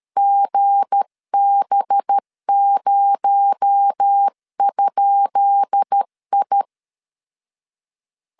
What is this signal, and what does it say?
Morse code, heard as a keyed tone, about 13 wpm, 800 Hz. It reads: GB0?I
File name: LEZIONI / 10 SIGNS / 10B INT 12wpm 2414